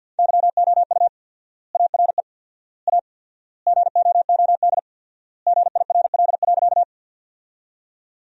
Morse code, heard as a keyed tone, about 50 wpm, 700 Hz. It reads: YOU ARE A GOOD GIRL.
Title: 4x Word-Spacing - Sentences from Top 400 Words - 50wpm